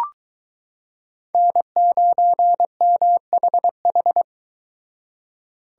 Morse code, heard as a keyed tone, about 22 wpm, 700 Hz. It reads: N9MHH